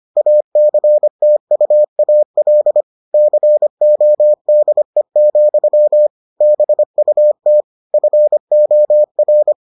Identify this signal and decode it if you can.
Morse code, heard as a keyed tone, about 25 wpm, 600 Hz. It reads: ACTUAL CODE, BUT FOR